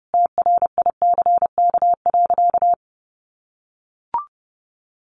Morse code, heard as a keyed tone, about 30 wpm, 700 Hz. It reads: TRICK.
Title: Sentences from Top 1500 Words 30wpm